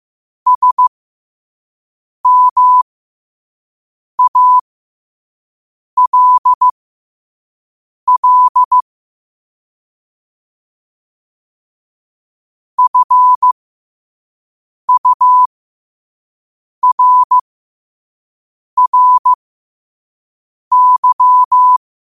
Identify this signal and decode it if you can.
Morse code, heard as a keyed tone, about 15 wpm, spaced as for 5 wpm, 1000 Hz. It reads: SMALL FURRY
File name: qotdfiles / 5wpm